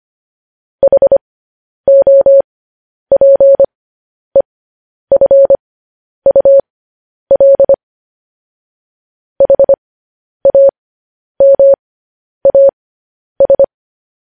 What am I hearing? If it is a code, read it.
Morse code, heard as a keyed tone, about 25 wpm, spaced as for 10 wpm, 550 Hz. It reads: HOPEFUL HAMAS